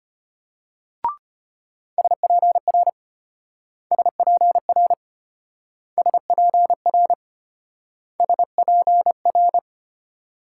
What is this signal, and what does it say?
Morse code, heard as a keyed tone, about 34 wpm, 700 Hz. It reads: SPR SPR SPR SPR